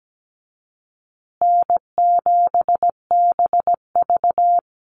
Morse code, heard as a keyed tone, about 17 wpm, 700 Hz. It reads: N7BV
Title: Easy Call Signs Repeated 2x - 17wpm